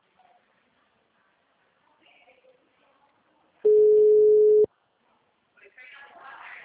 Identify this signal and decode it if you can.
Ringback tone (Europe, 425 Hz), with speech.
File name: out- -2011- - - .66927